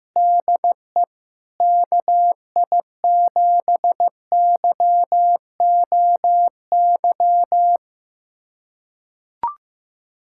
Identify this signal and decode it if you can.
Morse code, heard as a keyed tone, about 15 wpm, 700 Hz. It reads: DE KI7YOY
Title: USA Callsign DE Callsign - 6 Characters - 15wpm